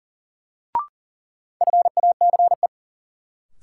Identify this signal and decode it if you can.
Morse code, heard as a keyed tone, about 39 wpm, 700 Hz. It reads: FACE
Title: Sentences from Top 300 Words - New Word Drill - 40wpm